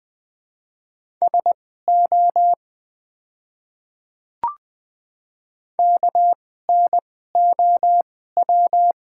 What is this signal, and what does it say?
Morse code, heard as a keyed tone, about 20 wpm, spaced as for 14 wpm, 700 Hz. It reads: SO KNOW
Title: 2x Character Spacing - Top 100 Words - 20wpm